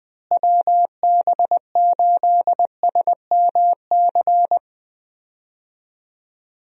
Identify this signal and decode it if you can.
Morse code, heard as a keyed tone, about 20 wpm, 700 Hz. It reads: WB8SMC
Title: USA Call Signs - 6 Characters - Repeated 2x - 20wpm